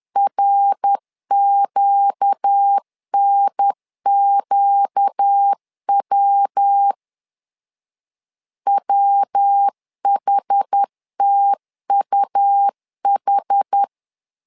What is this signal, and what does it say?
Morse code, heard as a keyed tone, about 10 wpm, 800 Hz. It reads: RQNQW WHTUH
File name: LEZIONI / 4 Q 10wpm 2210